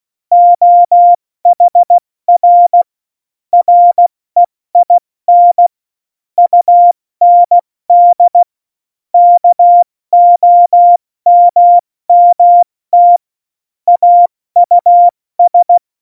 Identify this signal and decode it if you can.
Morse code, heard as a keyed tone, about 16 wpm, 700 Hz. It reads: OHR REIN UND KOMMT AUS
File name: Morsekurs / Witze / Witz 13 March 2018, 70 F80